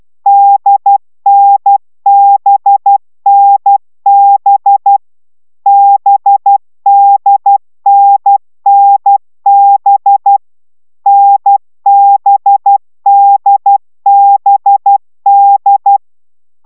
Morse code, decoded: DNBNB BDNNB NBDBD

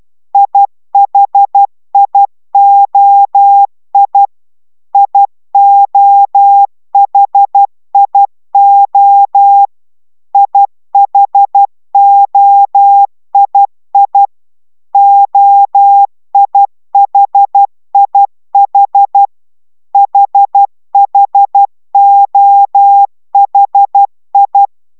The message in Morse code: IHIOI IOHIO IHOII OIHIH HHOHI

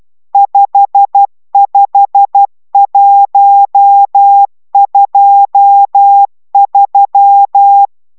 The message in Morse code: 55123

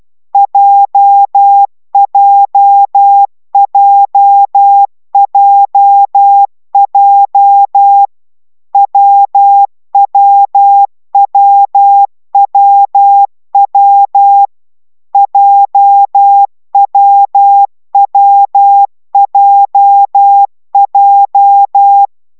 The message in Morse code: JJJJJ WWWWW JWWJJ

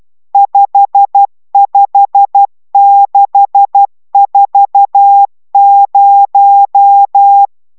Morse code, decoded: 55640

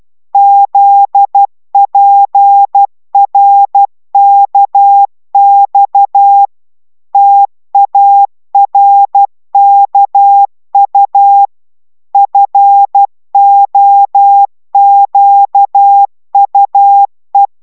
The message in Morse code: ZPRKX TARKU FOQUE

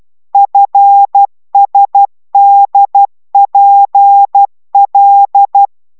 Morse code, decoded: FSDPL

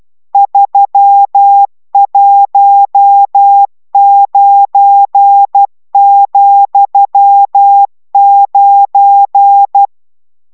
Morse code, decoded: 319,9

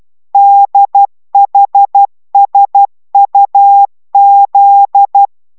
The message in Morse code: DHSUZ

